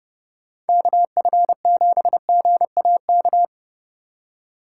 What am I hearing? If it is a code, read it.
Morse code, heard as a keyed tone, about 28 wpm, 700 Hz. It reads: KF7GAK